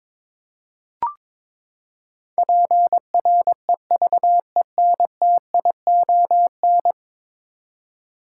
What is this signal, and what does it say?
Morse code, heard as a keyed tone, about 22 wpm, 700 Hz. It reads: PREVENTION
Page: morse-code-ninja-practice-22wpm